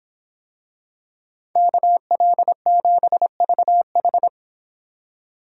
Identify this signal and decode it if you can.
Morse code, heard as a keyed tone, about 25 wpm, 700 Hz. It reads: KL7VH